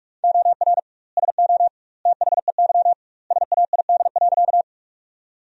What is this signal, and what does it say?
Morse code, heard as a keyed tone, about 42 wpm, 700 Hz. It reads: OR SO THEY SAID.
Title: 2x Word-Spacing - Sentences from Top 100 Words - 45wpm